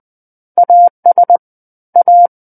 Morse code, heard as a keyed tone, about 20 wpm, 700 Hz. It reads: AS A
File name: 20 WPM